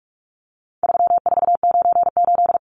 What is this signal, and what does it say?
Morse code, heard as a keyed tone, about 44 wpm, 700 Hz. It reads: 3498
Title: Numbers - 4-Digit 45wpm